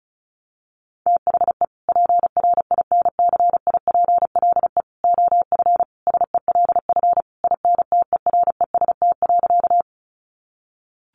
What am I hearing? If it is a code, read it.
Morse code, heard as a keyed tone, about 34 wpm, 700 Hz. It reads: THE PRINCIPLE OF SELF INTEREST.
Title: Sentences from Top 1400 Words 35wpm